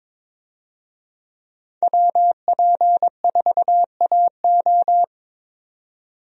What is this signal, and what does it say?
Morse code, heard as a keyed tone, about 22 wpm, 700 Hz. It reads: WP4AO
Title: USA Call Signs - 5 Characters - 22wpm